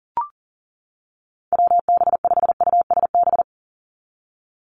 Morse code, heard as a keyed tone, about 42 wpm, 700 Hz. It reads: WB5USB
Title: NA SOTA Chaser Callsigns 40wpm